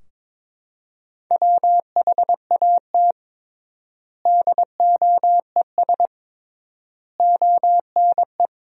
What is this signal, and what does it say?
Morse code, heard as a keyed tone, about 22 wpm, 700 Hz. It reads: WHAT DOES ONE
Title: 3x Word-Spacing - Sentences from Top 200 Words - 22wpm